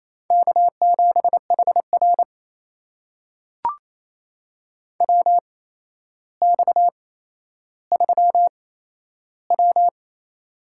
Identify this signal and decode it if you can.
Morse code, heard as a keyed tone, about 28 wpm, spaced as for 12 wpm, 700 Hz. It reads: K7HR WX3W
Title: USA Call Signs - 4 Characters - 8x Character Spacing - 28wpm